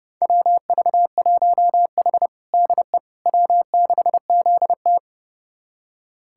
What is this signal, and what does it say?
Morse code, heard as a keyed tone, about 32 wpm, 700 Hz. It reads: WV1H DE W6ZT
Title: USA Callsign DE Callsign - 4 Characters - 30wpm